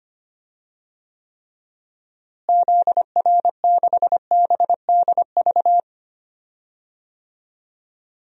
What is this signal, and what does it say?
Morse code, heard as a keyed tone, about 25 wpm, 700 Hz. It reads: ZR6BDV
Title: DXCC-100 Call Signs - 6 Characters - Repeated 2x - 25wpm